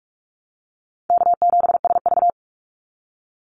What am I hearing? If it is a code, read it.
Morse code, heard as a keyed tone, about 44 wpm, 700 Hz. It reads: K7SV
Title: Easy Call Signs Repeated 2x - 45wpm